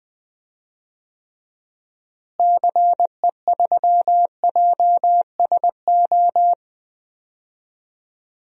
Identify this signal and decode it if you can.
Morse code, heard as a keyed tone, about 21 wpm, 700 Hz. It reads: CE3JSO
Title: DXCC-100 Call Signs - 6 Characters - Repeated 2x - 20wpm